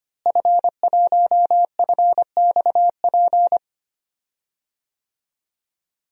Morse code, heard as a keyed tone, about 25 wpm, 700 Hz. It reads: F1FXP
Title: DXCC-100 Call Signs - 5 Characters - Repeated 3x - 25wpm